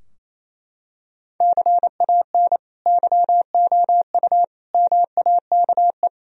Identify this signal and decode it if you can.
Morse code, heard as a keyed tone, about 28 wpm, 700 Hz. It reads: CAN YOU MAKE
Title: Sentences from Top 1000 Words - Review Part 4 - 28wpm